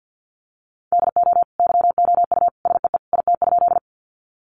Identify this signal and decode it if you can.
Morse code, heard as a keyed tone, about 50 wpm, 700 Hz. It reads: DO YOU SEE IT?
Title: Sentences from Top 100 Words 50wpm